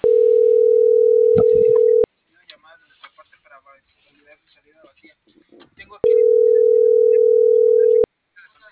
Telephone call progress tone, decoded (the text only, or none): ringback tone